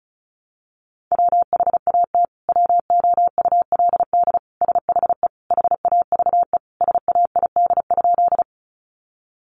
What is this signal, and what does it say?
Morse code, heard as a keyed tone, about 34 wpm, 700 Hz. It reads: WHAT WOULD SHE HAVE SAID?
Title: Sentences from Top 100 Words 35wpm